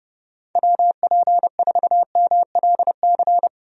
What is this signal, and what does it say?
Morse code, heard as a keyed tone, about 30 wpm, 700 Hz. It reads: WP4MLC
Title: USA Call Signs - 6 Characters - Repeated 2x - 30wpm